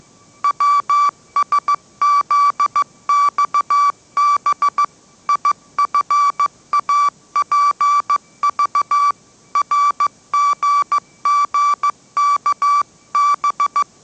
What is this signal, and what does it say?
Morse code, heard as a keyed tone, about 15 wpm, 1160 Hz. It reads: WSZXB IFAPV RGGKB